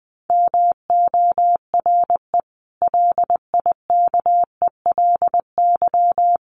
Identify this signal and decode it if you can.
Morse code, heard as a keyed tone, about 19 wpm, 700 Hz. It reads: MORE LIKELY
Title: Sets of 3 Words 20wpm